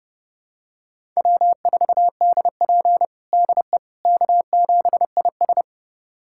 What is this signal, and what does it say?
Morse code, heard as a keyed tone, about 30 wpm, 700 Hz. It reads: W4DP DE K7IS
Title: USA Callsign DE Callsign - 4 Characters - 30wpm